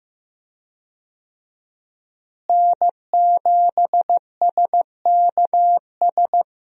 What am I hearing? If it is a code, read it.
Morse code, heard as a keyed tone, about 15 wpm, 700 Hz. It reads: N7SKS
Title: USA Call Signs - 5 Characters - Repeated 2x - 15wpm